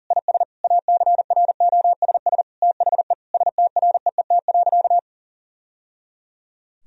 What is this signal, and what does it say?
Morse code, heard as a keyed tone, about 40 wpm, 700 Hz. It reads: IS ACROSS THE STREET.
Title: Sentences from Top 1000 Words - Review Part 4 - 40wpm